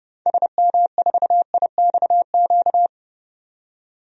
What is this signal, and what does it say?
Morse code, heard as a keyed tone, about 30 wpm, 700 Hz. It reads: SM4IXQ
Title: DXCC-100 Call Signs - 6 Characters - 30wpm